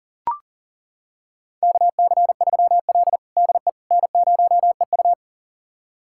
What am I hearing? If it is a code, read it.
Morse code, heard as a keyed tone, about 40 wpm, 700 Hz. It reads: KC3L DE N0EU
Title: USA Callsign DE Callsign - 4 Characters - 40wpm